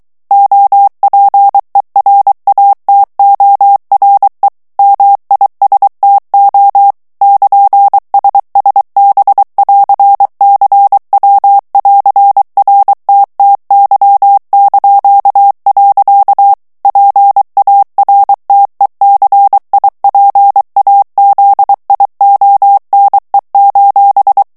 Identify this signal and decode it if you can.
Morse code, heard as a keyed tone, about 23 wpm, 800 Hz. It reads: OPERATORE MISTO <KN>SSB<AR>CW<AR>RTTY). PARTECIPAZIONE:
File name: ITALIEN texte clair 2 vitesse 22 mots minute